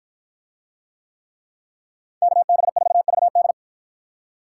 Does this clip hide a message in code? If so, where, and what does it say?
Yes: Morse code from 2.2 to 3.5 s, heard as a keyed tone, about 53 wpm, 700 Hz. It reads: KB4VD